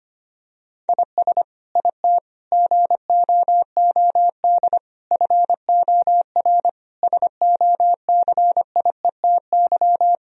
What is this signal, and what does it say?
Morse code, heard as a keyed tone, about 25 wpm, 700 Hz. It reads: IS IT GOOD FOR SOCIETY
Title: Sentences from Top 1000 Words - Review Part 4 - 25wpm